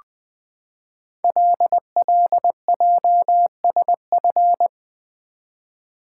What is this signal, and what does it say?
Morse code, heard as a keyed tone, about 20 wpm, 700 Hz. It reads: LLJSF
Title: Code Groups - 5 Letters - 20wpm